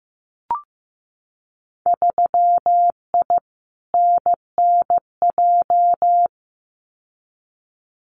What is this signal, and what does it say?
Morse code, heard as a keyed tone, about 15 wpm, 700 Hz. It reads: 3I NNJ
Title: Winter Field Day Exchanges 15wpm